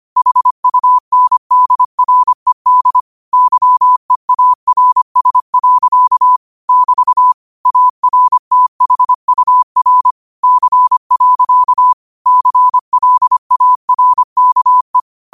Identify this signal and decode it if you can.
Morse code, heard as a keyed tone, about 25 wpm, 1000 Hz. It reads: HUNDRED YEARS. <BT> ARTHUR C. CLARKE
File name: qotdfiles / 25wpm